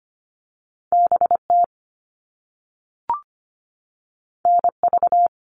Morse code, heard as a keyed tone, about 25 wpm, 700 Hz. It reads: BT NV
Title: All Cut Numbers - 2-Digit 25wpm